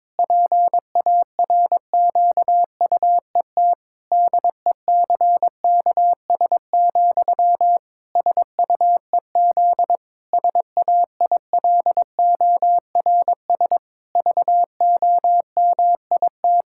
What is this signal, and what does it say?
Morse code, heard as a keyed tone, about 22 wpm, 700 Hz. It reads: PARQUET DECKS, SUEZ SAILORS VOMIT
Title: Pangram Challenge v2 - 22wpm